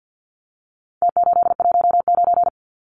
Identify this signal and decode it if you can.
Morse code, heard as a keyed tone, about 50 wpm, 700 Hz. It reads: T819
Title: Numbers - Contest Style - 4-Digit 50wpm